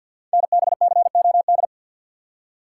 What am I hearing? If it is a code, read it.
Morse code, heard as a keyed tone, about 52 wpm, 700 Hz. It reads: NBCOD
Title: Code Groups - 5 Letters - 50wpm